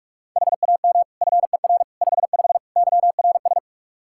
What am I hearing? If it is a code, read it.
Morse code, heard as a keyed tone, about 42 wpm, 700 Hz. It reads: HAM FER 55 YRS
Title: Practice QSO - Exchange 3 - More Info - 45wpm